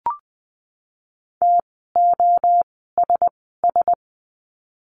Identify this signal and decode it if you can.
Morse code, heard as a keyed tone, about 20 wpm, 700 Hz. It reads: T O S S
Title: 061 Lesson 11f - Letter h - Words 20wpm